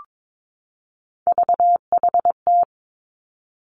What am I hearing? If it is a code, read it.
Morse code, heard as a keyed tone, about 22 wpm, 700 Hz. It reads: VHT
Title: Three Challenging Letter-Number Combinations 22wpm